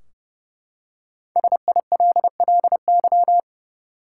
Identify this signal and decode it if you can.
Morse code, heard as a keyed tone, about 30 wpm, 700 Hz. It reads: SILLY
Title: Five Letter Words - 30wpm